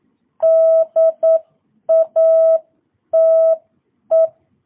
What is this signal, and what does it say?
Morse code, heard as a keyed tone, about 9 wpm, 660 Hz. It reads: DATE